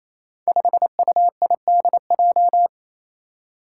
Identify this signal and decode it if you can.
Morse code, heard as a keyed tone, about 27 wpm, 700 Hz. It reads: 5UIDJ
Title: Five Challenging Letter-Number Combinations 28wpm